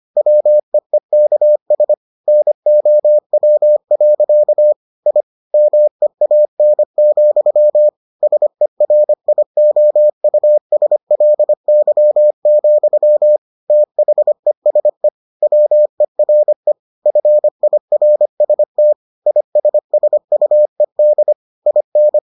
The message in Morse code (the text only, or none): WEEKS NOW. I MEAN, SERIOUSLY, THESE WERE FIRST ISSUED IN